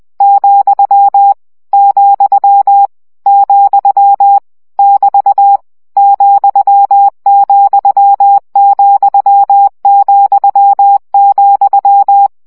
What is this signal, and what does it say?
Morse code, heard as a keyed tone, about 20 wpm, 800 Hz. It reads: , , , <BT> ,,,,,
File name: CW gyakorlas vesszo 17 baud